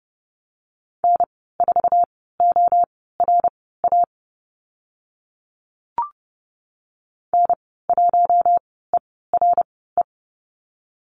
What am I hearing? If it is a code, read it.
Morse code, heard as a keyed tone, about 31 wpm, spaced as for 17 wpm, 700 Hz. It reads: N4ORA N1ERE
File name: Lesson 14g - Number 2 - Call Signs 30wpm